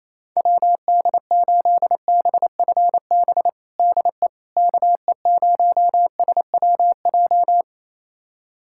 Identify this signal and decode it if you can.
Morse code, heard as a keyed tone, about 27 wpm, 700 Hz. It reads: WD8BFB DE KE0SWJ
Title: USA Callsign DE Callsign - 6 Characters - 28wpm